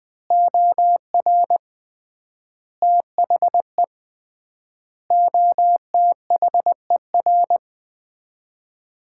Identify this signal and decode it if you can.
Morse code, heard as a keyed tone, about 20 wpm, 700 Hz. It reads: OR THE OTHER
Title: 3x Word-Spacing - Sets of 5 Words - 20wpm